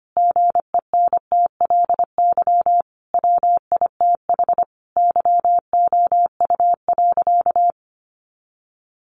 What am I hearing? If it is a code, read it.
Morse code, heard as a keyed tone, about 25 wpm, 700 Hz. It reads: GENTLY WITH YOU.
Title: Sentences from Top 1500 Words 25wpm